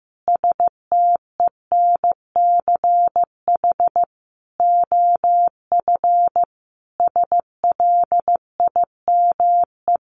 Morse code, heard as a keyed tone, about 15 wpm, 700 Hz. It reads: STENCH OF SLIME
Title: A Sound Of Thunder 15wpm